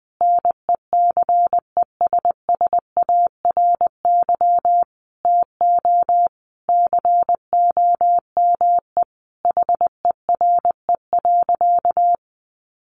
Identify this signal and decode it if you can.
Morse code, heard as a keyed tone, about 19 wpm, 700 Hz. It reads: NECESSARY TO COME HERE.
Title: Sentences from Top 1300 Words 20wpm